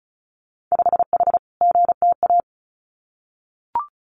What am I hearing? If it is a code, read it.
Morse code, heard as a keyed tone, about 34 wpm, 700 Hz. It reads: 5H GTA
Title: Winter Field Day Exchanges Encore 35wpm